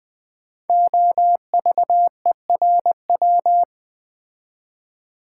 Morse code, heard as a keyed tone, about 20 wpm, 700 Hz. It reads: OVERW
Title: Build A Word - 5 Letters At A Time - 12-Letter Words - 20wpm